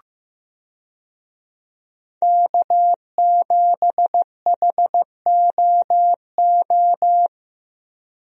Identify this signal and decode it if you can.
Morse code, heard as a keyed tone, about 15 wpm, 700 Hz. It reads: K7HOO
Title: USA Call Signs - 5 Characters - Repeated 3x - 15wpm